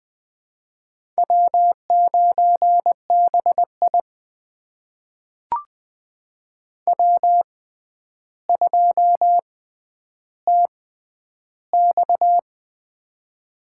Morse code, heard as a keyed tone, about 20 wpm, spaced as for 10 wpm, 700 Hz. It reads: W9BI W2TX
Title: USA Call Signs - 4 Characters - 6x Character Spacing - 20wpm